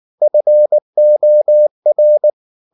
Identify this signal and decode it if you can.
Morse code, heard as a keyed tone, about 19 wpm, 600 Hz. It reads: FOR
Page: Ferie 19 wpm